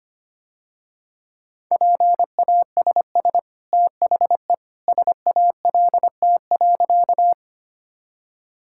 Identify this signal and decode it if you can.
Morse code, heard as a keyed tone, about 25 wpm, 700 Hz. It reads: PASS THE SALT.
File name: Easy Sentences with Hints - 25wpm